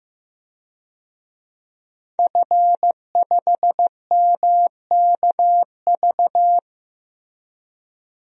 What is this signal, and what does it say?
Morse code, heard as a keyed tone, about 15 wpm, 700 Hz. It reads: F5MKV
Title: DXCC-100 Call Signs - 5 Characters - Repeated 3x - 15wpm